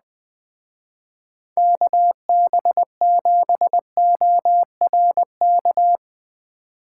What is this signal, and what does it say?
Morse code, heard as a keyed tone, about 20 wpm, 700 Hz. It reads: KB7ORK
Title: Speed Racing - USA 2x3 Call Signs 17wpm